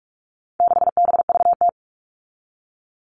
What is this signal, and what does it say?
Morse code, heard as a keyed tone, about 44 wpm, 700 Hz. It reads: 6BVT
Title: All Cut Numbers - 4-Digit 45wpm